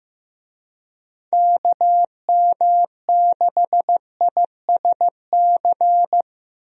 Morse code, heard as a keyed tone, about 15 wpm, 700 Hz. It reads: KM6ISC
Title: USA Call Signs - 6 Characters - Repeated 3x - 15wpm